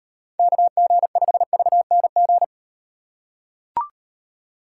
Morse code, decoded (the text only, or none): KG5VNG